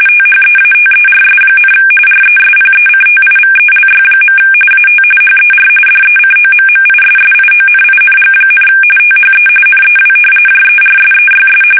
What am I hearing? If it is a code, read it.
RTTY, about 45.45 baud. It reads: THEATERS, RESPECTIVELY. AT HER DECOMMISSIONING, THE NEW JERSEYS
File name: fsk850 / battleshipnj